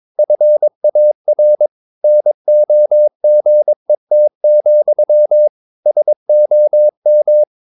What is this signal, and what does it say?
Morse code, heard as a keyed tone, about 22 wpm, 600 Hz. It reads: FAR NOGET, SOM
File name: Ferie QRS 22wpm